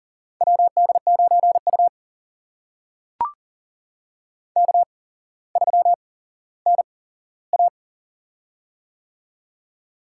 Morse code, heard as a keyed tone, about 40 wpm, spaced as for 17 wpm, 700 Hz. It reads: WD9U K3NA
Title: USA Call Signs - 4 Characters - 8x Character Spacing - 40wpm